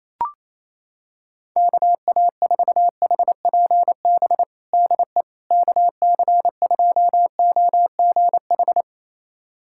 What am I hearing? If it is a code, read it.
Morse code, heard as a keyed tone, about 28 wpm, 700 Hz. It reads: KA4HPB DE KC2OGH
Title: USA Callsign DE Callsign 28wpm